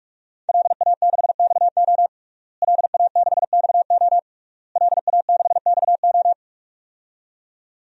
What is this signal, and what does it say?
Morse code, heard as a keyed tone, about 46 wpm, 700 Hz. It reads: LA6XO LA6XO LA6XO
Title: DXCC-100 Call Signs - 5 Characters - Repeated 3x - 45wpm